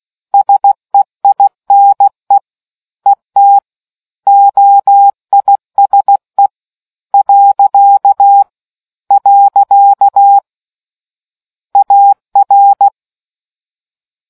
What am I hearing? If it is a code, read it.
Morse code, heard as a keyed tone, about 16 wpm, 800 Hz. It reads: SEINE ET OISE . AR